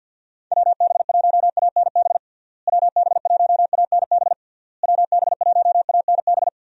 Morse code, heard as a keyed tone, about 52 wpm, 700 Hz. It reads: WB1ANB WB1ANB WB1ANB
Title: USA Call Signs - 6 Characters - Repeated 3x - 50wpm